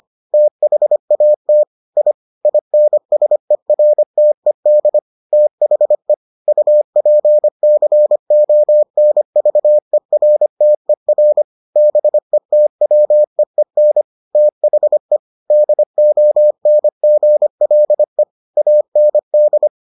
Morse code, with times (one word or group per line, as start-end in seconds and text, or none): 0.3-1.6 THAT
2.0-2.1 I
2.5-5.0 INSERTED
5.3-6.1 THE
6.5-11.4 UPCONVERTER
11.8-14.0 BETWEEN
14.3-15.2 THE
15.5-18.2 DONGLE
18.6-19.7 AND